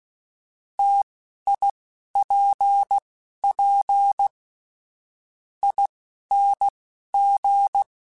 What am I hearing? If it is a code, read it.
Morse code, heard as a keyed tone, about 16 wpm, spaced as for 12 wpm, 800 Hz. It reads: TIPP ING
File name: Difficult Combos / Difficult Combos-mp3s / T AOEI @10